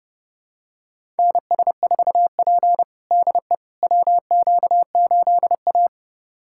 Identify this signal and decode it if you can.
Morse code, heard as a keyed tone, about 30 wpm, 700 Hz. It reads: NS4P DE WQ8A
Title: USA Callsign DE Callsign - 4 Characters - 30wpm